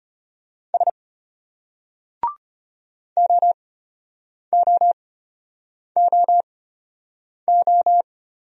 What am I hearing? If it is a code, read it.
Morse code, heard as a keyed tone, about 33 wpm, spaced as for 8 wpm, 700 Hz. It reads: S OOOO